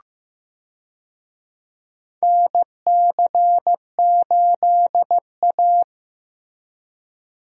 Morse code, heard as a keyed tone, about 15 wpm, 700 Hz. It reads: NC8A
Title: USA Call Signs - 4 Characters - Repeated 3x - 15wpm